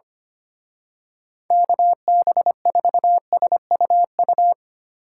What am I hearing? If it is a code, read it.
Morse code, heard as a keyed tone, about 27 wpm, 700 Hz. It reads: KB4SUU
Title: Speed Racing - USA 2x3 Call Signs 25wpm